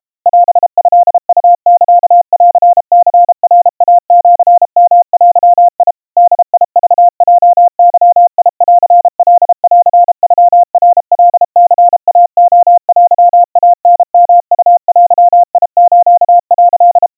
Morse code (Japanese Wabun code), heard as a keyed tone, about 32 wpm, 700 Hz. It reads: オトウサンニナイシヨデ ボクヲゲンカンノナカニイレテイタヨウデス」
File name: kawa / Morse sample / kamyu 120 cpm 700 Hz Crispy